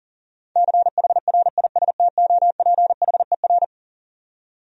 Morse code, heard as a keyed tone, about 39 wpm, 700 Hz. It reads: CHRISTOPHER
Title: USA Common Names - 40wpm